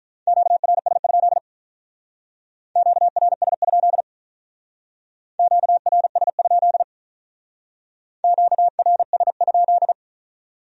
Morse code, heard as a keyed tone, about 45 wpm, 700 Hz. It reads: QRS? QRS? QRS? QRS?